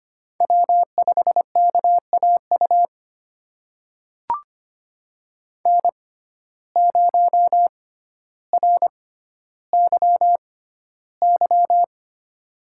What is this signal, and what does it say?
Morse code, heard as a keyed tone, about 25 wpm, spaced as for 13 wpm, 700 Hz. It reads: W5KAU N0RYY
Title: USA Call Signs - 5 Characters - 6x Character Spacing - 25wpm